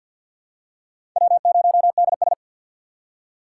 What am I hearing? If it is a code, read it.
Morse code, heard as a keyed tone, about 47 wpm, 700 Hz. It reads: W0DS
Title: USA Call Signs - 4 Characters - 6x Character Spacing - 50wpm